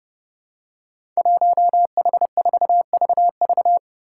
Morse code, heard as a keyed tone, about 30 wpm, 700 Hz. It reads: 1H4VV
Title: Five Challenging Letter-Number Combinations 30wpm